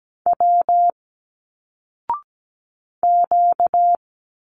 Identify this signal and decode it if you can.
Morse code, heard as a keyed tone, about 17 wpm, 700 Hz. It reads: W Q
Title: Single Letters 17wpm